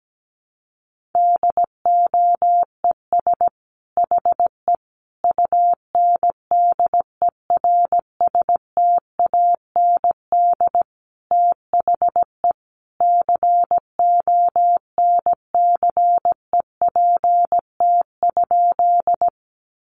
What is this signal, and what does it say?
Morse code, heard as a keyed tone, about 17 wpm, 700 Hz. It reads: DOES HE UNDERSTAND THE CONCEPT?
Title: Sentences from Top 1100 Words 17wpm